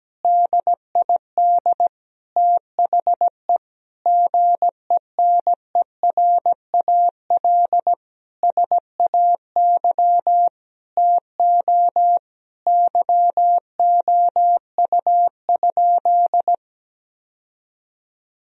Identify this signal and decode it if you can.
Morse code, heard as a keyed tone, about 17 wpm, 700 Hz. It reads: DID THE GENERAL SAY TO YOU?
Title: Sentences from Top 1000 Words - Review Part 3 - 17wpm